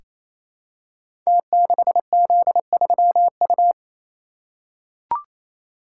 Morse code, heard as a keyed tone, about 28 wpm, 700 Hz. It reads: T6Z3U
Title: Code Groups - 5 Letter:Numbers - 28wpm